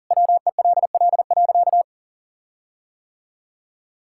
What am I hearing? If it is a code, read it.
Morse code, heard as a keyed tone, about 40 wpm, 700 Hz. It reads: WELL.
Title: Sentences from Top 1600 Words 40wpm